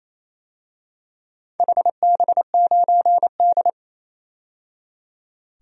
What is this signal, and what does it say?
Morse code, heard as a keyed tone, about 29 wpm, 700 Hz. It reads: HB9D
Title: DXCC-100 Call Signs - 4 Characters - 28wpm